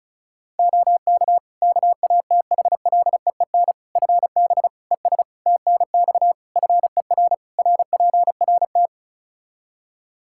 Morse code, heard as a keyed tone, about 36 wpm, 700 Hz. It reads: OK KATHLEEN FB ES TNX FER RPRT
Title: Practice QSO - Exchange 2 - Info - 35wpm